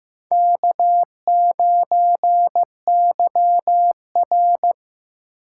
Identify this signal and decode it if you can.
Morse code, heard as a keyed tone, about 18 wpm, 700 Hz. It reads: K9YR
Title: Speed Racing - USA Call Signs 15wpm